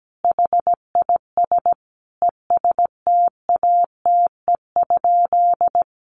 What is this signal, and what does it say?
Morse code, heard as a keyed tone, about 17 wpm, 700 Hz. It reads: HIS ESTATE?
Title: Sentences from Top 1200 Words 17wpm